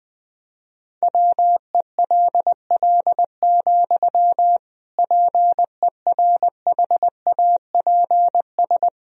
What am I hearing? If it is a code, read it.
Morse code, heard as a keyed tone, about 21 wpm, 700 Hz. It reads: WELL, PERHAPS
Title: Sentences from Top 500 Words and Abbreviations 20wpm